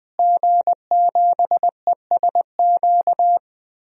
Morse code, heard as a keyed tone, about 20 wpm, 700 Hz. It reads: G7ESQ
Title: DXCC-100 Call Signs - 5 Characters - Repeated 3x - 20wpm